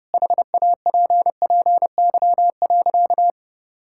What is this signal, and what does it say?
Morse code, heard as a keyed tone, about 32 wpm, 700 Hz. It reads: HAPPY.
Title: Sentences from Top 1000 Words - Review Part 5 - 30wpm